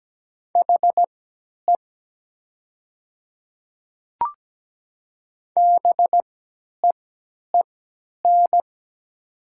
Morse code, heard as a keyed tone, about 17 wpm, spaced as for 10 wpm, 700 Hz. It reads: HE BEEN
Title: 3x Character Spacing - Top 100 Words - 17wpm